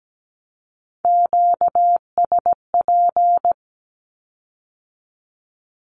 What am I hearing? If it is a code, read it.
Morse code, heard as a keyed tone, about 17 wpm, 700 Hz. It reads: QSP
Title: Q-Codes with Spoken Meaning - 17wpm